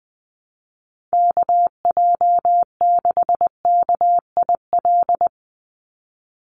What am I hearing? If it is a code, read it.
Morse code, heard as a keyed tone, about 20 wpm, 700 Hz. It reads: KJ6KIL
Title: USA Call Signs 20wpm